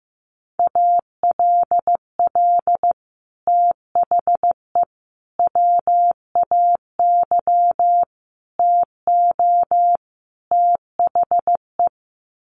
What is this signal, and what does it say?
Morse code, heard as a keyed tone, about 15 wpm, 700 Hz. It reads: ALL THE WAY TO THE
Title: Sets of 5 Words 15wpm